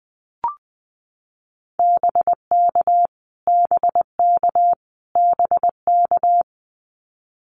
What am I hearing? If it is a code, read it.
Morse code, heard as a keyed tone, about 20 wpm, 700 Hz. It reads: BK BK BK
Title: Abbreviations Repeated 3x - 20wpm